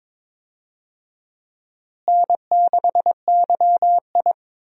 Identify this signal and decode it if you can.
Morse code, heard as a keyed tone, about 22 wpm, 700 Hz. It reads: N6YI
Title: USA Call Signs - 4 Characters - Repeated 2x - 22wpm